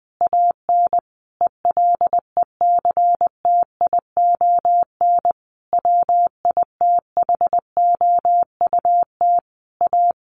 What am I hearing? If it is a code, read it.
Morse code, heard as a keyed tone, about 20 wpm, 700 Hz. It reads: AN ELECTION WITHOUT A
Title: Sentences from Top 1500 Words 20wpm